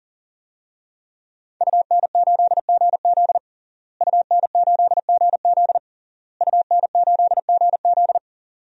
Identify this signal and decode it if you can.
Morse code, heard as a keyed tone, about 40 wpm, 700 Hz. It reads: UN8GZ UN8GZ UN8GZ
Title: DXCC-100 Call Signs - 5 Characters - Repeated 3x - 40wpm